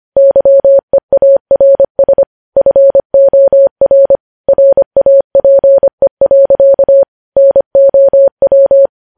Morse code, heard as a keyed tone, about 25 wpm, 550 Hz. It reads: YEARS FOR RAPE. NOW
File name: podcast / News headlines-1x-25x25